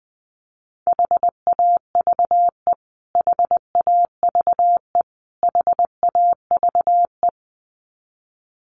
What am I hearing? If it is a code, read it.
Morse code, heard as a keyed tone, about 20 wpm, 700 Hz. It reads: HAVE HAVE HAVE
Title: Top 200 Words Repeated 20wpm